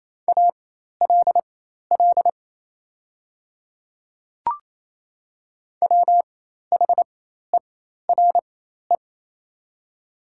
Morse code, heard as a keyed tone, about 28 wpm, spaced as for 13 wpm, 700 Hz. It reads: ALL WHERE
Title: 4x Character Spacing - Top 300 Words - 28wpm